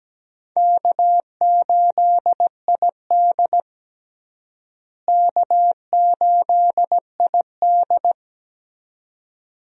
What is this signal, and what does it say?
Morse code, heard as a keyed tone, about 17 wpm, 700 Hz. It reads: K8ID K8ID
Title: USA Call Signs - 4 Characters - Repeated 2x - 17wpm